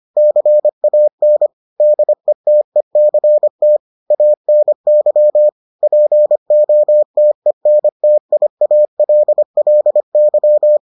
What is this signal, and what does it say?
Morse code, heard as a keyed tone, about 25 wpm, 600 Hz. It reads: CAN DETECT ANY POTENTIALLY